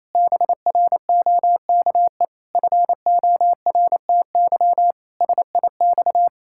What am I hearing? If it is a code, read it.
Morse code, heard as a keyed tone, about 27 wpm, 700 Hz. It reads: BROKE FORTY SIX
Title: Pangram Challenge v2 - 28wpm